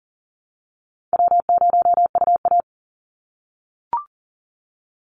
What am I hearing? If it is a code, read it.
Morse code, heard as a keyed tone, about 40 wpm, 700 Hz. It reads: W0UA
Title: Easy Call Signs - 40wpm